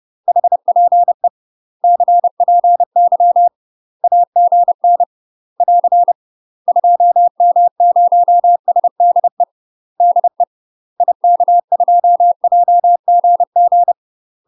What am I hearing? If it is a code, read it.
Morse code, heard as a keyed tone, about 30 wpm, 700 Hz. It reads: HPE CPY AGN <AR> 2M0SDE DE IK2JGG
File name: UK MASTER QSOs TRAINING / 30WPM / UK MASTER QSOs 0027